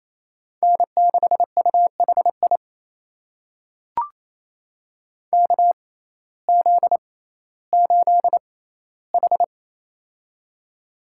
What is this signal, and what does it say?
Morse code, heard as a keyed tone, about 28 wpm, spaced as for 15 wpm, 700 Hz. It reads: N6UHI KZ8H
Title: USA Call Signs - 6x Character Spacing - 28wpm